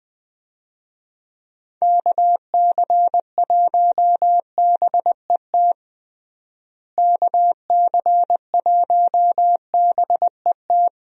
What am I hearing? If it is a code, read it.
Morse code, heard as a keyed tone, about 20 wpm, 700 Hz. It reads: KC1BET KC1BET
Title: USA Call Signs - 6 Characters - Repeated 2x - 20wpm